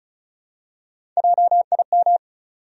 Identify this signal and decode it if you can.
Morse code, heard as a keyed tone, about 35 wpm, 700 Hz. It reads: JIM
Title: USA Common Names - Max 4 Letters - 35wpm